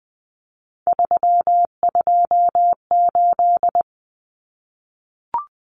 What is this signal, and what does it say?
Morse code, heard as a keyed tone, about 20 wpm, 700 Hz. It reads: 328
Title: Common Cut Numbers - 3-Digit 20wpm